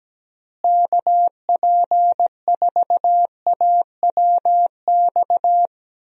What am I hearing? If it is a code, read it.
Morse code, heard as a keyed tone, about 17 wpm, 700 Hz. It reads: KP4AWX
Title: DXCC-100 Call Signs - 6 Characters - Repeated 3x - 17wpm